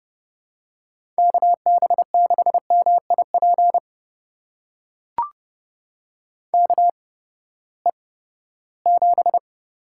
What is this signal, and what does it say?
Morse code, heard as a keyed tone, about 32 wpm, spaced as for 17 wpm, 700 Hz. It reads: KB6MIP KE7
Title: USA Call Signs - 6 Characters - 8x Character Spacing - 30wpm